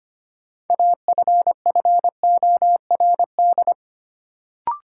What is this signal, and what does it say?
Morse code, heard as a keyed tone, about 25 wpm, 700 Hz. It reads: AFFORD
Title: Sentences from Top 1100 Words - New Word Drill Alt - 25wpm